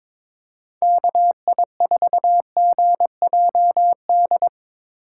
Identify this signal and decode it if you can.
Morse code, heard as a keyed tone, about 22 wpm, 700 Hz. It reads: KI4GJD
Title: USA Call Signs - 6 Characters - 3x Character Spacing - 22wpm